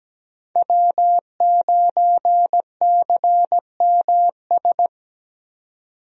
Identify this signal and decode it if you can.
Morse code, heard as a keyed tone, about 17 wpm, 700 Hz. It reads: W9CMS
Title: USA Call Signs - 5 Characters - 17wpm